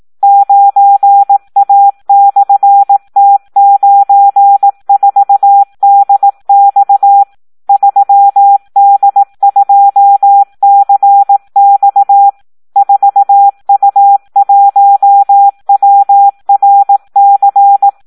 Morse code, heard as a keyed tone, about 18 wpm, 800 Hz. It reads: 9A/T94DX 3D2CX 4U1WRC
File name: Audio practica CW / SD 1 18